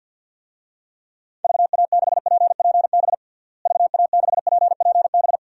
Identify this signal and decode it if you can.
Morse code, heard as a keyed tone, about 47 wpm, 700 Hz. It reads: VA6PPB VA6PPB
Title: DXCC-100 Call Signs - 6 Characters - Repeated 2x - 50wpm